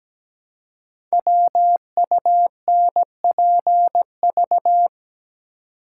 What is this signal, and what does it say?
Morse code, heard as a keyed tone, about 17 wpm, 700 Hz. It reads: WUNPV
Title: Code Groups - 5 Letters - 17wpm